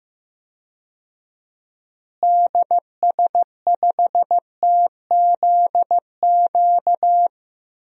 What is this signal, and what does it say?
Morse code, heard as a keyed tone, about 15 wpm, 700 Hz. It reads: DS5TZQ
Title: DXCC-100 Call Signs - 6 Characters - Repeated 3x - 15wpm